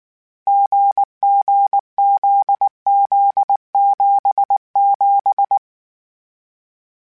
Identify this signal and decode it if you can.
Morse code, heard as a keyed tone, about 19 wpm, 800 Hz. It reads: GGZZ77
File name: CW Code Groups-mp3s / Sets mp3s / GZ GZ7 sp @18